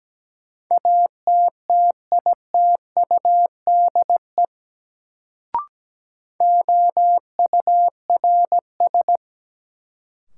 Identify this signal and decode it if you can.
Morse code, heard as a keyed tone, about 17 wpm, 700 Hz. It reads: ATTITUDE OURS
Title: Sentences from Top 1300 Words - New Word Drill Alt - 17wpm